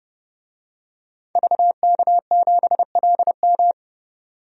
Morse code, heard as a keyed tone, about 31 wpm, 700 Hz. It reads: VK7LM
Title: DXCC-100 Call Signs - 5 Characters - 30wpm